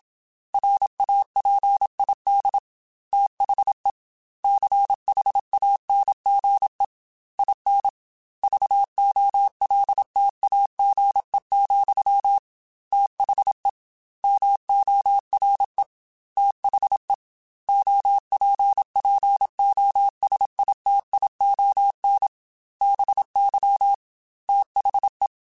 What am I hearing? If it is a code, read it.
Morse code, heard as a keyed tone, about 27 wpm, 800 Hz. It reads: RAPID THE CHANGE IN VOLTAGE, THE MORE THE OPPOSITION BY THE
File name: Technical Subjects / Technical Subjects-mp3s / XC @25